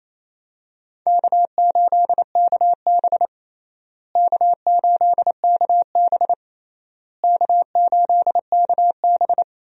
Morse code, heard as a keyed tone, about 28 wpm, 700 Hz. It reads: K8KB K8KB K8KB
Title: USA Call Signs - 4 Characters - Repeated 3x - 28wpm